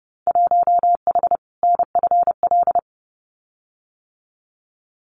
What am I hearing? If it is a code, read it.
Morse code, heard as a keyed tone, about 28 wpm, 700 Hz. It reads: 1H NFL
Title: Winter Field Day Exchanges 30wpm